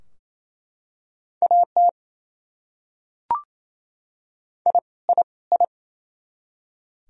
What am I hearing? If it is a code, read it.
Morse code, heard as a keyed tone, about 28 wpm, 700 Hz. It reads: AT I I I
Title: Top 100 Words Repeated 28wpm